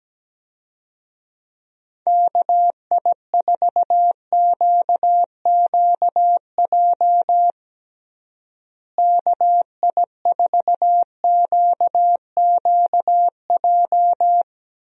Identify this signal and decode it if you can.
Morse code, heard as a keyed tone, about 17 wpm, 700 Hz. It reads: KI4QQJ KI4QQJ
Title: USA Call Signs - 6 Characters - Repeated 3x - 17wpm